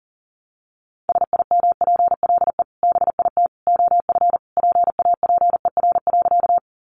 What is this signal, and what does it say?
Morse code, heard as a keyed tone, about 40 wpm, 700 Hz. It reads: SIMPLE BIT OF PAPER.
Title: Sentences from Top 900 Words 40wpm